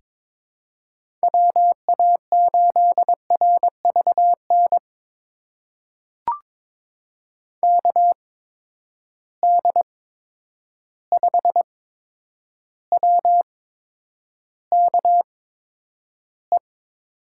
Morse code, heard as a keyed tone, about 22 wpm, spaced as for 9 wpm, 700 Hz. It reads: WA8RVN KD5WKE
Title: USA Call Signs - 6 Characters - 8x Character Spacing - 22wpm